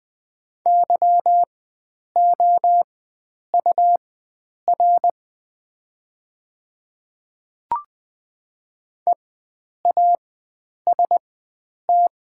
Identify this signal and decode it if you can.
Morse code, heard as a keyed tone, about 20 wpm, spaced as for 9 wpm, 700 Hz. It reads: YOUR EAST
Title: 4x Character Spacing - Top 500 Words - 20wpm